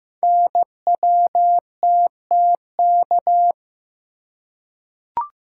Morse code, heard as a keyed tone, about 15 wpm, 700 Hz. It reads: NWTTK
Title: Code Groups - 5 Letters - 15wpm